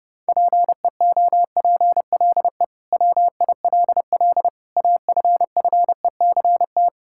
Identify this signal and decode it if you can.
Morse code, heard as a keyed tone, about 32 wpm, 700 Hz. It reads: PEOPLE WILL AFFECT
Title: Sentences from Top 1000 Words - Review Part 6 - 30wpm